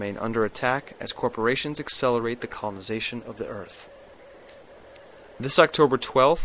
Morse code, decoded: none

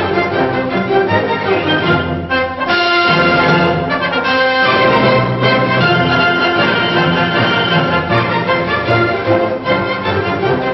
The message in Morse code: none

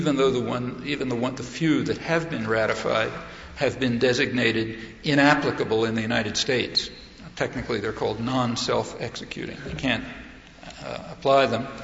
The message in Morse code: none

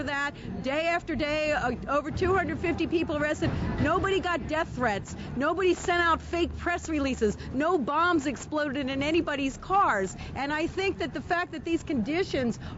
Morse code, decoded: none

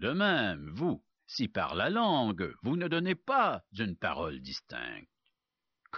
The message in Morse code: none

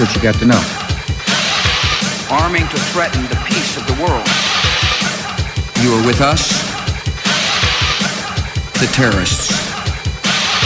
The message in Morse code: none

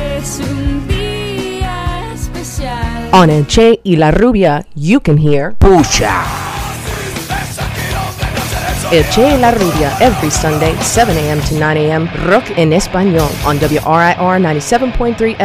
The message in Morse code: none

